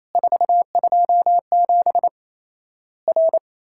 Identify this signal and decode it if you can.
Morse code, heard as a keyed tone, about 28 wpm, 700 Hz. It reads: 427 R
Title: Worked All Europe DX Contest - QTC Exchange - 28wpm